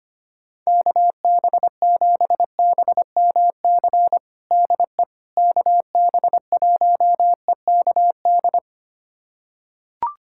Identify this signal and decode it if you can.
Morse code, heard as a keyed tone, about 25 wpm, 700 Hz. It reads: KB7BMC DE KB1EKD
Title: USA Callsign DE Callsign - 6 Characters - 25wpm